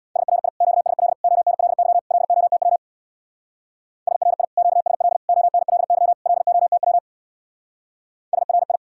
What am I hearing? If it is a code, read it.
Morse code, heard as a keyed tone, about 60 wpm, 700 Hz. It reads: SRI QSL ONLY LOTW SRI QSL ONLY LOTW SRI
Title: Speed-Racing Common QSO Phrases Encore - 50wpm